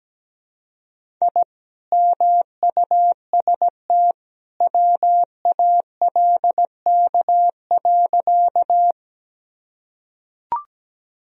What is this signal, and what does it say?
Morse code, heard as a keyed tone, about 17 wpm, 700 Hz. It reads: I MUST WALK.
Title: Sentences from Top 1000 Words - Review Part 1 - 17wpm